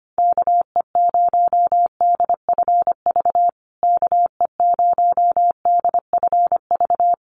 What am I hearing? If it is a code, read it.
Morse code, heard as a keyed tone, about 25 wpm, 700 Hz. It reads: KE0DFV KE0DFV
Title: USA Call Signs Repeated 3x 25wpm